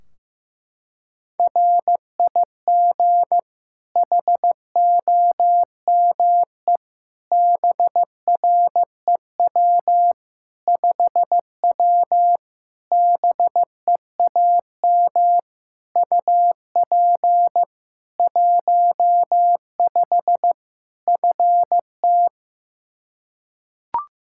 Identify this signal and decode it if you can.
Morse code, heard as a keyed tone, about 15 wpm, 700 Hz. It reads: RIG HOME BREW 5W BEAM UP 15 FT
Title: K1USN SST QSO Practice 15wpm